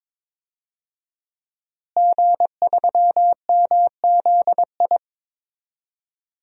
Morse code, heard as a keyed tone, about 21 wpm, 700 Hz. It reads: G3MZI
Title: DXCC-100 Call Signs - 5 Characters - Repeated 3x - 22wpm